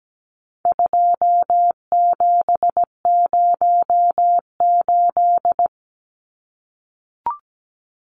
Morse code, heard as a keyed tone, about 17 wpm, 700 Hz. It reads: 2708
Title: Numbers - 4-Digit 17wpm